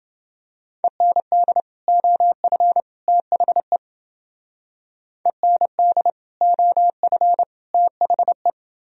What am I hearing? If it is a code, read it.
Morse code, heard as a keyed tone, about 29 wpm, 700 Hz. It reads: END OF THE END OF THE